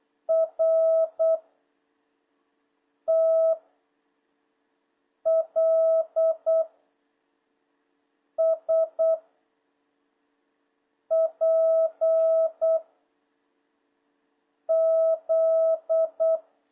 Morse code, heard as a keyed tone, about 8 wpm, 660 Hz. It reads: R T L S P Z
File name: jh8cbh / 2026cw